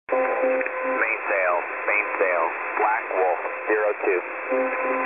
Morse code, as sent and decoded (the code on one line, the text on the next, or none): none